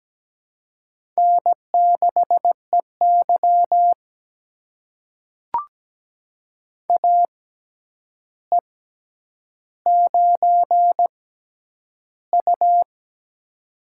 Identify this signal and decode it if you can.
Morse code, heard as a keyed tone, about 17 wpm, spaced as for 9 wpm, 700 Hz. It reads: N6EY AE9U
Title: USA Call Signs - 4 Characters - 6x Character Spacing - 17wpm